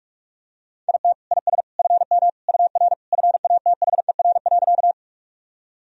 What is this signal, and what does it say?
Morse code, heard as a keyed tone, about 46 wpm, 700 Hz. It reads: IT IS FM UR FATHER.
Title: Sentences from Top 300 Words and Abbreviations 45wpm